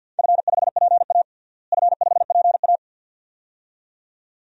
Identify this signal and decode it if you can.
Morse code, heard as a keyed tone, about 50 wpm, 700 Hz. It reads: F5PA F5PA
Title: DXCC-100 Call Signs - 4 Characters - Repeated 2x - 50wpm